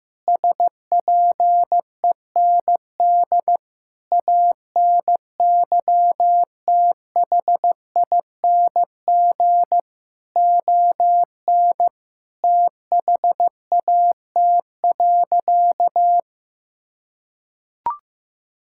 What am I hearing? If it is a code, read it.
Morse code, heard as a keyed tone, about 15 wpm, 700 Hz. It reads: SPEND ANYTHING ON THAT.
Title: Sentences from Top 1000 Words - Review Part 5 - 15wpm